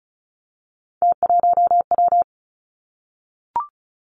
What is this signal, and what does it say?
Morse code, heard as a keyed tone, about 36 wpm, 700 Hz. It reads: T1W
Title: Three Challenging Letter-Number Combinations 35wpm